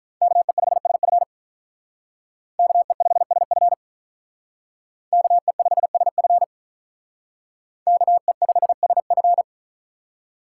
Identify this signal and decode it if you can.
Morse code, heard as a keyed tone, about 44 wpm, 700 Hz. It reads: KE5SF KE5SF KE5SF KE5SF